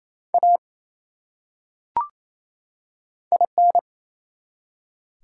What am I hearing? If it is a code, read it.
Morse code, heard as a keyed tone, about 28 wpm, 700 Hz. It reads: A IN
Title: Top 180 QSO Elements 28wpm